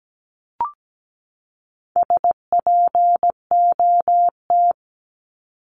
Morse code, heard as a keyed tone, about 17 wpm, 700 Hz. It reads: SPOT
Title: Top 1001-1250 Words 17wpm